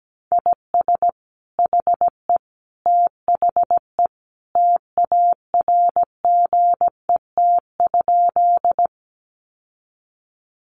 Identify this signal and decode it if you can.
Morse code, heard as a keyed tone, about 17 wpm, 700 Hz. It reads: IS HE THE TARGET?
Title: Sentences from Top 1200 Words 17wpm